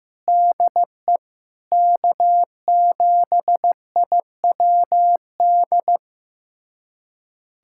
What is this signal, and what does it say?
Morse code, heard as a keyed tone, about 15 wpm, 700 Hz. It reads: DE K7IWD
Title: USA Callsign DE Callsign - 5 Characters - 15wpm